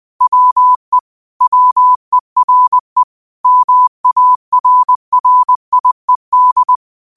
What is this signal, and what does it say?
Morse code, heard as a keyed tone, about 20 wpm, 1000 Hz. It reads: WE WERE MARRIED
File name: qotdfiles / 20wpm